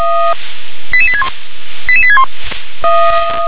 Five-tone selective call (ZVEI), heard as quick runs of five tones, tones of about 70 ms. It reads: 88861 88861